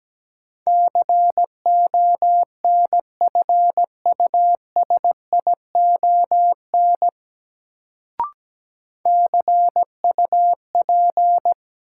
Morse code, heard as a keyed tone, about 17 wpm, 700 Hz. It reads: CONFUSION CUP
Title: Sentences from Top 1600 Words - New Word Drill - 17wpm